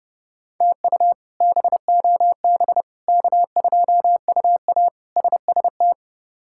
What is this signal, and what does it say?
Morse code, heard as a keyed tone, about 30 wpm, 700 Hz. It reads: TU BOB K2UA SST
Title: K1USN SST QSO Practice - 30wpm